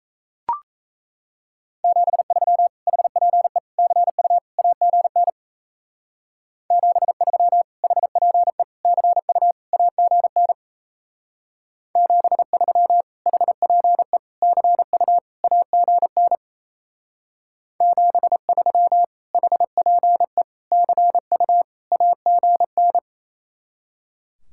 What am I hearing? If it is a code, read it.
Morse code, heard as a keyed tone, about 36 wpm, 700 Hz. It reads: 73 HPE CU AGN 73 HPE CU AGN 73 HPE CU AGN 73 HPE CU AGN